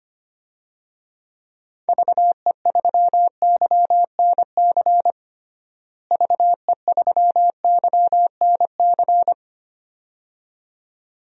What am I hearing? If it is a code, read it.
Morse code, heard as a keyed tone, about 25 wpm, 700 Hz. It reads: VE3YNC VE3YNC
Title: DXCC-100 Call Signs - 6 Characters - Repeated 2x - 25wpm